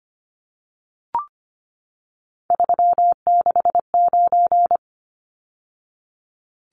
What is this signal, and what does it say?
Morse code, heard as a keyed tone, about 25 wpm, 700 Hz. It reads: 369
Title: Numbers - 3-Digit 25wpm